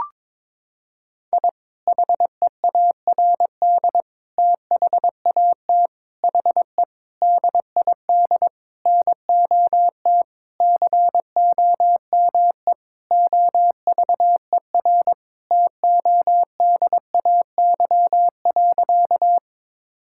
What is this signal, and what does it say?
Morse code, heard as a keyed tone, about 22 wpm, 700 Hz. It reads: I HEARD THAT HE DID NOT COME OVER TODAY.
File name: practice / 22wpm / Sentences with 4-Word Phrases - 22wpm